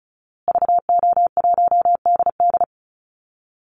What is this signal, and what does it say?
Morse code, heard as a keyed tone, about 34 wpm, 700 Hz. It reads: VO1DD
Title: Call Signs Repeated 2x - 35wpm